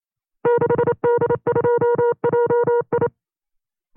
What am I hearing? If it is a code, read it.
Morse code, heard as a keyed tone, about 29 wpm, 480 Hz. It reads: 6D2JI
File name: CW-practice/25WPM